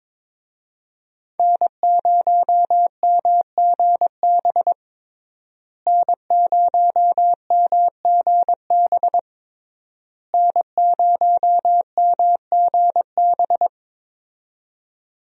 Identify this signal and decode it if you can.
Morse code, heard as a keyed tone, about 21 wpm, 700 Hz. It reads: N0MGB N0MGB N0MGB
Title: USA Call Signs - 5 Characters - Repeated 3x - 22wpm